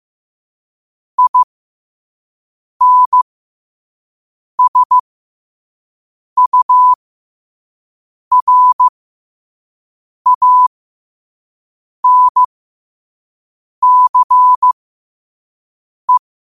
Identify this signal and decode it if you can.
Morse code, heard as a keyed tone, about 15 wpm, spaced as for 5 wpm, 1000 Hz. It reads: INSURANCE
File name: qotdfiles / 5wpm